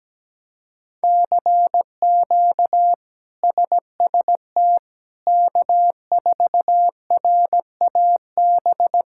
Morse code, heard as a keyed tone, about 17 wpm, 700 Hz. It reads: CQ SST K4RAB
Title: K1USN SST QSO Practice - 17wpm